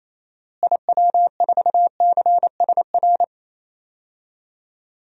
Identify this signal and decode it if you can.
Morse code, heard as a keyed tone, about 28 wpm, 700 Hz. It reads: IW4CSR
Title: DXCC-100 Call Signs - 6 Characters - Repeated 3x - 28wpm